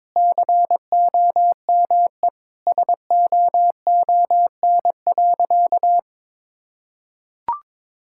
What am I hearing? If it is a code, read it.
Morse code, heard as a keyed tone, about 21 wpm, 700 Hz. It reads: COME SOON.
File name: practice / 22wpm / Sentences with 2-Word Phrases - 22wpm